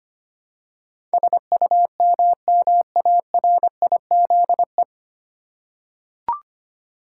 Morse code, heard as a keyed tone, about 25 wpm, 700 Hz. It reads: SUMMARIZE
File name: practice / 25wpm / 9 Letter Words - First 3 Letters Given - 25wpm